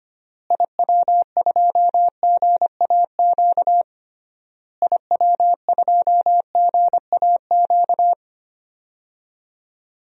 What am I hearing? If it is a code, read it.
Morse code, heard as a keyed tone, about 25 wpm, 700 Hz. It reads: IW2GAQ IW2GAQ
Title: DXCC-100 Call Signs - 6 Characters - Repeated 3x - 25wpm